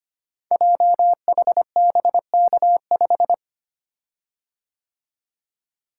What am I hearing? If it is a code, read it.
Morse code, heard as a keyed tone, about 25 wpm, 700 Hz. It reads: JHBK5
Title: Code Groups - 5 Letter:Numbers - 25wpm